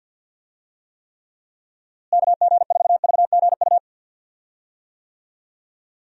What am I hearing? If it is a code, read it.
Morse code, heard as a keyed tone, about 51 wpm, 700 Hz. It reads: KG4VGU
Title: USA Call Signs - 6 Characters - Repeated 2x - 50wpm